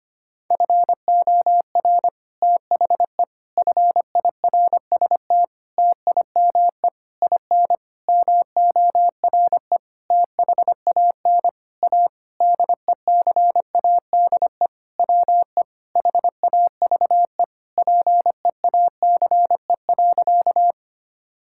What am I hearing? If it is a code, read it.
Morse code, heard as a keyed tone, about 25 wpm, 700 Hz. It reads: FOR THE FIRST TIME IN MORE THAN A DECADE WE HAVE PEACE.
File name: practice / 25wpm / Sentences with 9-Word Phrases - 25wpm